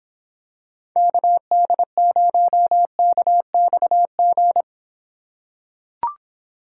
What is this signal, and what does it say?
Morse code, heard as a keyed tone, about 26 wpm, 700 Hz. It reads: KD0KXG